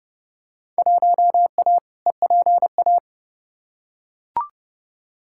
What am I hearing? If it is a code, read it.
Morse code, heard as a keyed tone, about 32 wpm, 700 Hz. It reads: 1A EPA
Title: Field Day Exchanges - 30wpm